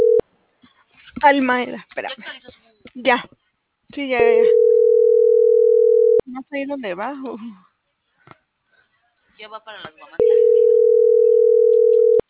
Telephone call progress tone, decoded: ringback tone

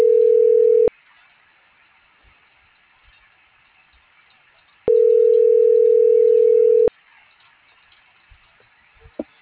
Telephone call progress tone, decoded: ringback tone

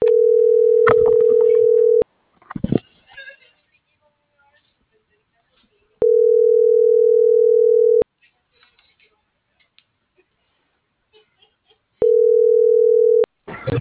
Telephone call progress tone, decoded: ringback tone

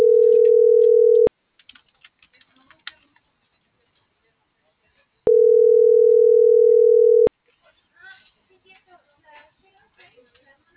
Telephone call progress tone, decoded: ringback tone

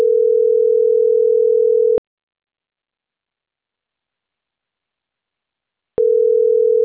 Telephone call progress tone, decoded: ringback tone

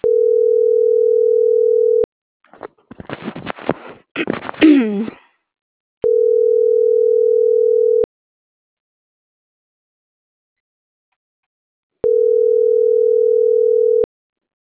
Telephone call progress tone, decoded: ringback tone

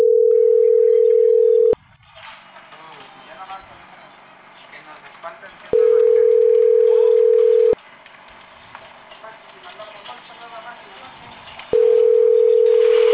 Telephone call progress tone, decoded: ringback tone

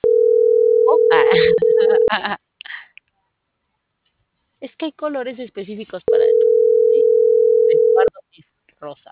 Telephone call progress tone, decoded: ringback tone

